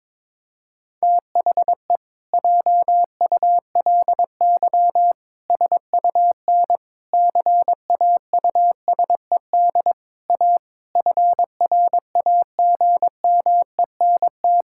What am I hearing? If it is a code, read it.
Morse code, heard as a keyed tone, about 22 wpm, 700 Hz. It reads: THE JULY SUN CAUSED A FRAGMENT
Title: Pangram Challenge v2 - 22wpm